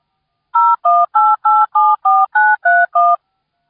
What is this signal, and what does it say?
DTMF, heard as telephone touch tones, tones of about 220 ms. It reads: 018874931